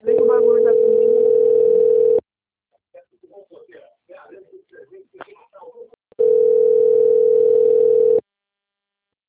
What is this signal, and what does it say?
Ringback tone (North America), 440 + 480 Hz, with speech.
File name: out- - - - - .73890